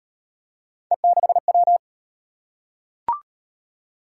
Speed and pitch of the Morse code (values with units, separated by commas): 33 wpm, 700 Hz